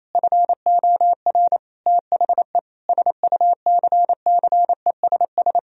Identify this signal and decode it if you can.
Morse code, heard as a keyed tone, about 28 wpm, 700 Hz. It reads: FOR THE SUCCESS
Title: Sentences from Top 1000 Words - Review Part 4 - 28wpm